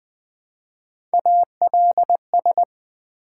Morse code, heard as a keyed tone, about 20 wpm, 700 Hz. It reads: ALS
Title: Build A Word - 1 Letter At A Time - 4-Letter Words - 20wpm